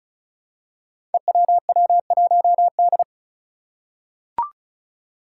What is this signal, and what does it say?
Morse code, heard as a keyed tone, about 36 wpm, 700 Hz. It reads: EWW1D
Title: Five Challenging Letter-Number Combinations 35wpm